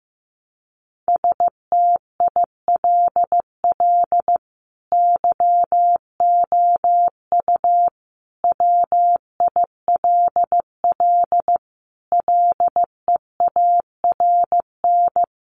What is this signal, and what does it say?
Morse code, heard as a keyed tone, about 15 wpm, 700 Hz. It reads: STILL YOU WILL LEARN
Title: Sentences from Top 600 Words 15wpm